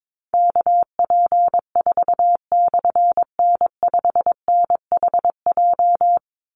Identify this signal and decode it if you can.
Morse code, heard as a keyed tone, about 22 wpm, 700 Hz. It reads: KP4/N5NHJ
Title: Call Signs Repeated 3x - 22wpm